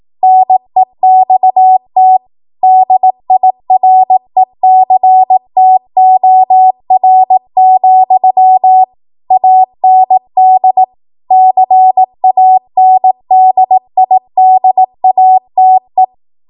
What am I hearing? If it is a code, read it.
Morse code, heard as a keyed tone, about 18 wpm, 750 Hz. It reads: NEXT DIRECTOR, AND CANDIDATE